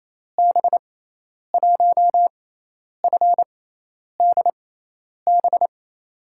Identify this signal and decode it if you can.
Morse code, heard as a keyed tone, about 28 wpm, spaced as for 10 wpm, 700 Hz. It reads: B1FDB
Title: USA Call Signs - 6 Characters - 6x Character Spacing - 28wpm